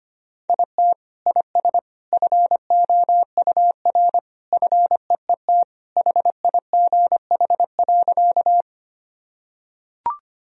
Morse code, heard as a keyed tone, about 25 wpm, 700 Hz. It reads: IT IS FOUR FEET HIGH.
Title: Sentences from Top 1000 Words - Review Part 1 - 25wpm